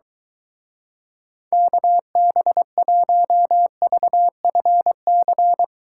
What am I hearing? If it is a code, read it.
Morse code, heard as a keyed tone, about 23 wpm, 700 Hz. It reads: KB1VFC